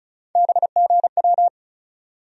Morse code, heard as a keyed tone, about 36 wpm, 700 Hz. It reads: BGW
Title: USA Call Sign Postfixes - 3 Letters 35wpm